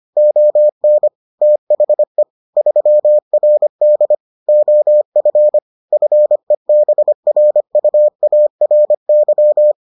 Morse code, heard as a keyed tone, about 25 wpm, 600 Hz. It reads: ON THE 3RD OF FEBRUARY